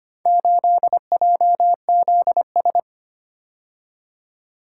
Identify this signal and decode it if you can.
Morse code, heard as a keyed tone, about 25 wpm, 700 Hz. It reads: 8JZS
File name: practice / 25wpm / Four Letter-Number Combinations - 25wpm